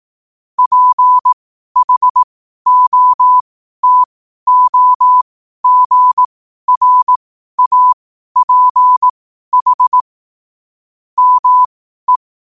Morse code, heard as a keyed tone, about 18 wpm, spaced as for 12 wpm, 1000 Hz. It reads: PHOTOGRAPH ME
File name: qotdfiles / 12wpm